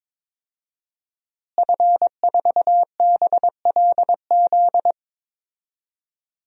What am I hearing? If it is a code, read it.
Morse code, heard as a keyed tone, about 22 wpm, 700 Hz. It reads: F4BLZ
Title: DXCC-100 Call Signs - 5 Characters - Repeated 3x - 22wpm